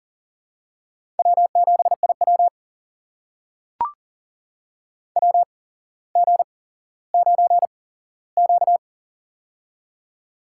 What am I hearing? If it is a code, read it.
Morse code, heard as a keyed tone, about 40 wpm, spaced as for 17 wpm, 700 Hz. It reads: W7IW WG9Q
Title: USA Call Signs - 4 Characters - 8x Character Spacing - 40wpm